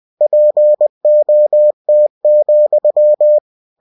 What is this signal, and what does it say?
Morse code, heard as a keyed tone, about 20 wpm, 600 Hz. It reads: POT,
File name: cw-books / winnie-the-pooh / winnie-20wpm-page-0057